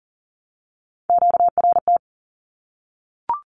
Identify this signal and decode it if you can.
Morse code, heard as a keyed tone, about 40 wpm, 700 Hz. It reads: QRT
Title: Q-Codes 40wpm